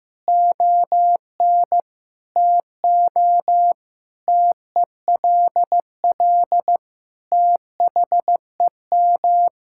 Morse code, heard as a keyed tone, about 15 wpm, 700 Hz. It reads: ON TO TELL THEM
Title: Sentences from Top 1000 Words - Review Part 6 - 15wpm